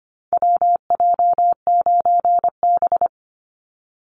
Morse code, heard as a keyed tone, about 25 wpm, 700 Hz. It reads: WJ9B
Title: Call Signs - 25wpm